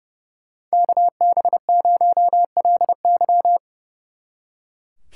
Morse code, heard as a keyed tone, about 29 wpm, 700 Hz. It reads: KB0LY
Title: Speed Racing - USA 2x2 Call Signs 30wpm